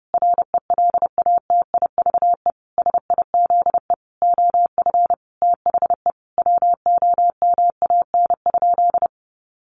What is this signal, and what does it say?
Morse code, heard as a keyed tone, about 30 wpm, 700 Hz. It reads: RELATIVE SIZE OF THE WOMAN?
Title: Sentences from Top 1400 Words 30wpm